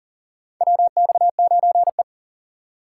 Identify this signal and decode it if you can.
Morse code, heard as a keyed tone, about 37 wpm, 700 Hz. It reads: WX9E
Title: USA Call Signs - 4x Character Spacing - 40wpm